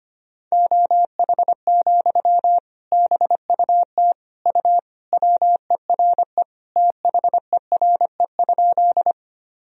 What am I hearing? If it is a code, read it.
Morse code, heard as a keyed tone, about 25 wpm, 700 Hz. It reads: OH, BUT U WERE THERE?
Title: Sentences from Top 500 Words and Abbreviations 25wpm